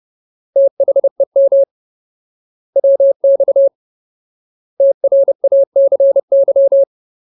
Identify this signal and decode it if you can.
Morse code, heard as a keyed tone, about 30 wpm, 550 Hz. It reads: THEM WX TRACY
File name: Ultimate Word List / 30WPM Ultimate